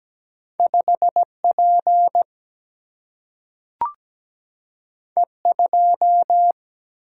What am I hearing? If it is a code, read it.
Morse code, heard as a keyed tone, about 17 wpm, 700 Hz. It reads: 5 P E 2
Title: Single-Letter Single-Number 17wpm